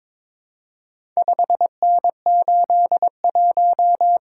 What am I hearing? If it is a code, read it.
Morse code, heard as a keyed tone, about 22 wpm, 700 Hz. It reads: 5N81
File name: practice / 22wpm / Four Letter-Number Combinations - 22wpm